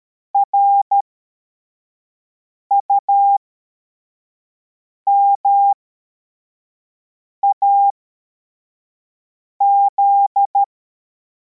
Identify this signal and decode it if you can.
Morse code, heard as a keyed tone, about 13 wpm, spaced as for 4 wpm, 800 Hz. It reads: RUMAZ
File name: formation / cw / Text42